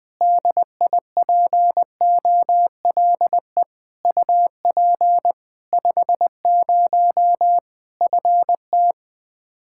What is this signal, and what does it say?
Morse code, heard as a keyed tone, about 20 wpm, 700 Hz. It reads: DIPOLE UP 50 FT
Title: Practice QSO - Exchange 2 - Info - 20wpm